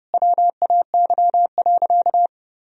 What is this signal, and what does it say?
Morse code, heard as a keyed tone, about 30 wpm, 700 Hz. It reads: WAY.
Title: Sentences from Top 1000 Words - Review Part 3 - 30wpm